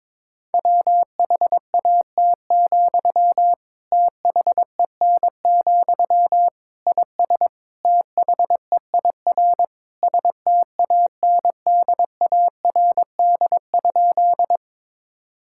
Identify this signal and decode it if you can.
Morse code, heard as a keyed tone, about 23 wpm, 700 Hz. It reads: WHAT, THEN, IS THEIR STANDARD?
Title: Sentences from Top 1000 Words - Review Part 6 - 22wpm